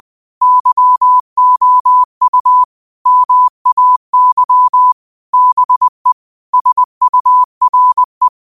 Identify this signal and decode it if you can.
Morse code, heard as a keyed tone, about 20 wpm, 1000 Hz. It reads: YOU MAY BE SURE
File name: qotdfiles / 20wpm